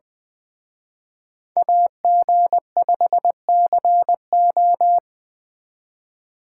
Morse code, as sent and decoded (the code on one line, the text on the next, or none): .- --. ..... -.-. ---
AG5CO